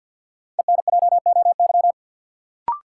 Morse code, heard as a keyed tone, about 48 wpm, 700 Hz. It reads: ENJOY
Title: Sentences from Top 1300 Words - New Word Drill Alt - 50wpm